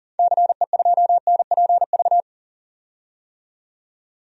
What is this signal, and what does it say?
Morse code, heard as a keyed tone, about 40 wpm, 700 Hz. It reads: CE2NPV
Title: DXCC-100 Call Signs - 6 Characters - 40wpm